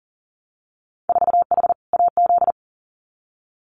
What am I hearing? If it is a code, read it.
Morse code, heard as a keyed tone, about 41 wpm, 700 Hz. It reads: 4H AZ
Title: Winter Field Day Exchanges Encore 40wpm